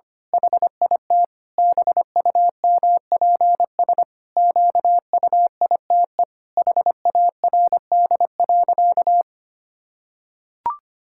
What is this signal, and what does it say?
Morse code, heard as a keyed tone, about 25 wpm, 700 Hz. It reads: HIT BUMPS QUITE HARD.
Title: Pangram Challenge v2 - 25wpm